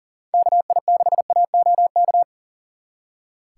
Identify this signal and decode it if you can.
Morse code, heard as a keyed tone, about 52 wpm, 700 Hz. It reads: KI6AOK